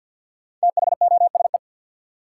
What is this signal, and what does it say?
Morse code, heard as a keyed tone, about 50 wpm, 700 Hz. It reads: THOSE
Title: Sets of 2 Words - New Word Drill - 50wpm